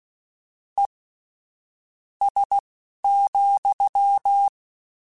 Morse code, heard as a keyed tone, about 16 wpm, spaced as for 7 wpm, 800 Hz. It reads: ES,
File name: General / General-mp3s / Suffixes sp @10